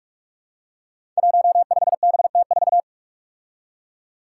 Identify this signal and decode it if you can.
Morse code, heard as a keyed tone, about 45 wpm, 700 Hz. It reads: 15BT4
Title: Five Challenging Letter-Number Combinations 45wpm